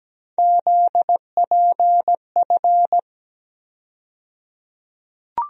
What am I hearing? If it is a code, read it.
Morse code, heard as a keyed tone, about 17 wpm, 700 Hz. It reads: ZPF
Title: USA Call Sign Postfixes - 3 Letters 17wpm